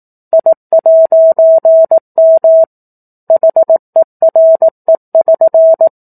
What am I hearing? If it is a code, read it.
Morse code, heard as a keyed tone, about 18 wpm, 650 Hz. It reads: I'M HERE<SN>
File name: 04-Chapter-04 18 WPM